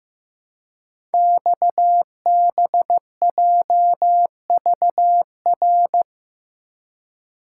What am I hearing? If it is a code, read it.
Morse code, heard as a keyed tone, about 15 wpm, 700 Hz. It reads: XBJVR
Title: Code Groups - 5 Letters - 15wpm